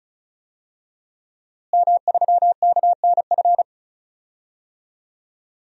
Morse code, heard as a keyed tone, about 34 wpm, 700 Hz. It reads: M3KNF
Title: DXCC-100 Call Signs - 5 Characters - Repeated 2x - 35wpm